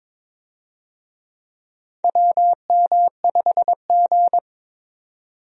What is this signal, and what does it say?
Morse code, heard as a keyed tone, about 22 wpm, 700 Hz. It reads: WM5G
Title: USA Call Signs - 4 Characters - 22wpm